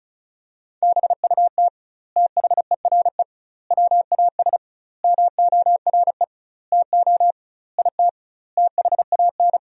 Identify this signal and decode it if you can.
Morse code, heard as a keyed tone, about 36 wpm, 700 Hz. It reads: BUT THERE WAS MORE TO IT THAN
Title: 2x Word-Spacing - Sentences from Top 100 Words - 35wpm